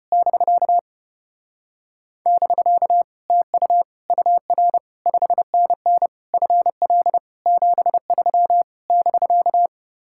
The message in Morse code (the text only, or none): <BK> <BK> TU UR 5NN FL 73 <BK>